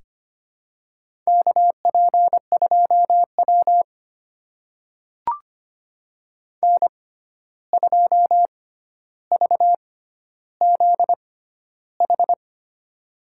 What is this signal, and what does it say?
Morse code, heard as a keyed tone, about 25 wpm, spaced as for 12 wpm, 700 Hz. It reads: KP2W N2VZH
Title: USA Call Signs - 6x Character Spacing - 25wpm